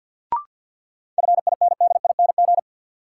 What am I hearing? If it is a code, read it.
Morse code, heard as a keyed tone, about 52 wpm, 700 Hz. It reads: FINDING
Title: Sentences from Top 1100 Words - New Word Drill Alt - 50wpm